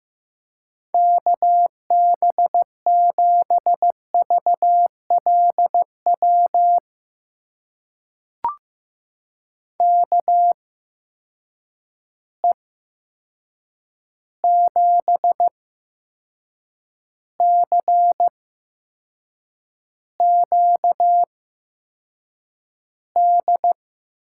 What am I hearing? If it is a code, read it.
Morse code, heard as a keyed tone, about 15 wpm, spaced as for 6 wpm, 700 Hz. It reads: KB7VLW KE7CQD
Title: USA Call Signs - 6 Characters - 8x Character Spacing - 15wpm